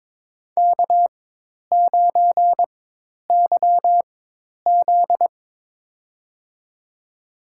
Morse code, heard as a keyed tone, about 22 wpm, spaced as for 10 wpm, 700 Hz. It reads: K9YZ
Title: USA Call Signs - 4 Characters - 4x Character Spacing - 22wpm